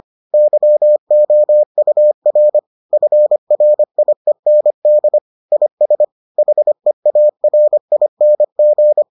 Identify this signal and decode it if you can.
Morse code, heard as a keyed tone, about 25 wpm, 600 Hz. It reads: YOUR FRIEND IS HEARING